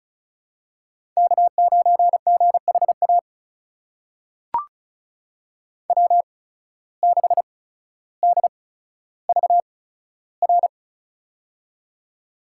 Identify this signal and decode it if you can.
Morse code, heard as a keyed tone, about 36 wpm, spaced as for 15 wpm, 700 Hz. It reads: K9GHA W6DVR
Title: USA Call Signs - 5 Characters - 8x Character Spacing - 35wpm